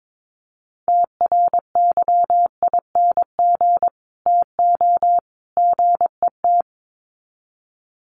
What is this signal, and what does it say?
Morse code, heard as a keyed tone, about 22 wpm, 700 Hz. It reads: TRYING TO GET
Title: Sets of 3 Words 22wpm